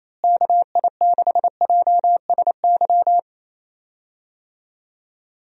Morse code, heard as a keyed tone, about 28 wpm, 700 Hz. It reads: KI6JSY
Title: USA Call Signs - 6 Characters - Repeated 2x - 28wpm